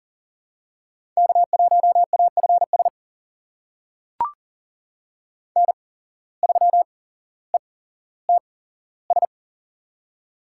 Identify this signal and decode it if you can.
Morse code, heard as a keyed tone, about 37 wpm, spaced as for 17 wpm, 700 Hz. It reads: K1AFS N3ETS
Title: USA Call Signs - 5 Characters - 8x Character Spacing - 40wpm